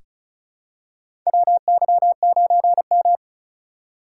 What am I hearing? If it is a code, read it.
Morse code, heard as a keyed tone, about 36 wpm, 700 Hz. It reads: WY9M
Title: USA Call Signs - 2x Character Spacing - 35wpm